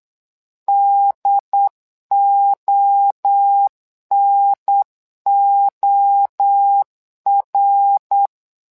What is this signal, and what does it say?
Morse code, heard as a keyed tone, about 8 wpm, 800 Hz. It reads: DONOR
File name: LEZIONI / 3 MWRO / 3 MWRO 8wpm 2606